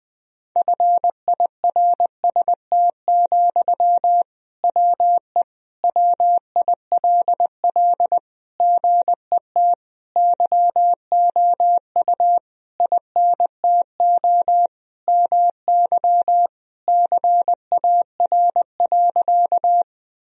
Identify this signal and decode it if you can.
Morse code, heard as a keyed tone, about 20 wpm, 700 Hz. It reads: FIRST, WE WILL GET YOU INTO MY CAR.
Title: Sentences from Top 1000 Words - Review Part 1 - 20wpm